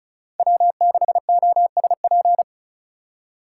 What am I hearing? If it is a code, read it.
Morse code, heard as a keyed tone, about 34 wpm, 700 Hz. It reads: W6OSP
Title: K1USN SST QSO Practice - 35wpm